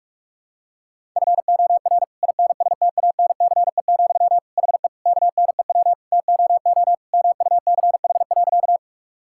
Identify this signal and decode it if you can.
Morse code, heard as a keyed tone, about 41 wpm, 700 Hz. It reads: FOR INSTANCE, HE KNEW TOO MUCH.
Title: Sentences from Top 1000 Words - Review Part 5 - 45wpm